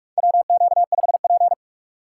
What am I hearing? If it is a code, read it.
Morse code, heard as a keyed tone, about 46 wpm, 700 Hz. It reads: WQ5P
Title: USA Call Signs - 4 Characters - 6x Character Spacing - 45wpm